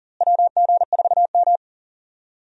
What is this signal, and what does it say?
Morse code, heard as a keyed tone, about 40 wpm, 700 Hz. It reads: WG4M
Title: USA Call Signs - 4 Characters - Repeated 2x - 40wpm